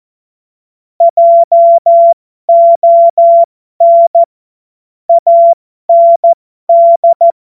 Morse code, heard as a keyed tone, about 14 wpm, spaced as for 12 wpm, 680 Hz. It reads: JON AND
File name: Morsekurs / Witze